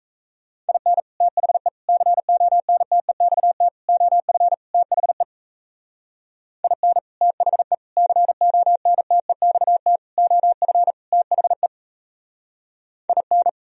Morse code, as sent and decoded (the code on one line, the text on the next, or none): .. -. / - .... / -.-. --- -. - . -..- - / --- ..-. / - .... / .. -. / - .... / -.-. --- -. - . -..- - / --- ..-. / - .... / .. -.
IN THE CONTEXT OF THE IN THE CONTEXT OF THE IN